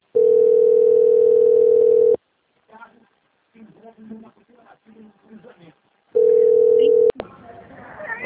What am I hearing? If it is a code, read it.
Ringback tone (North America), 440 + 480 Hz, with speech.
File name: out- - - - - .12223